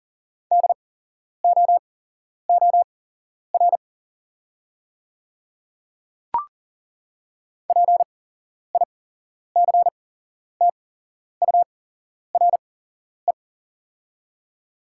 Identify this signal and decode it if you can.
Morse code, heard as a keyed tone, about 40 wpm, spaced as for 11 wpm, 700 Hz. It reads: DOOR PICTURE
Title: 8x Character Spacing - Top 300 Words - 40wpm